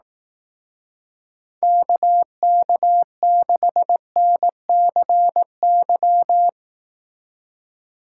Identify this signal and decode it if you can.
Morse code, heard as a keyed tone, about 18 wpm, 700 Hz. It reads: KK6NCY
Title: Speed Racing - USA 2x3 Call Signs 15wpm